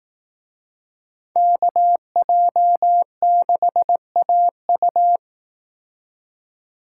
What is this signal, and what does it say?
Morse code, heard as a keyed tone, about 18 wpm, 700 Hz. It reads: KJ6AU
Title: Speed Racing - USA Call Signs 15wpm